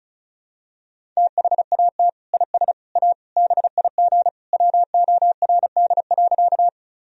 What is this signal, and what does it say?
Morse code, heard as a keyed tone, about 34 wpm, 700 Hz. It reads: THAT IS A BIG WORD.
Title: Sentences from Top 1000 Words - Review Part 1 - 35wpm